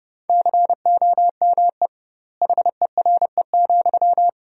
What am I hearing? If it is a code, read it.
Morse code, heard as a keyed tone, about 30 wpm, 700 Hz. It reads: COME HERE,
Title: 2x Word-Spacing - Sentences from Top 300 Words - 30wpm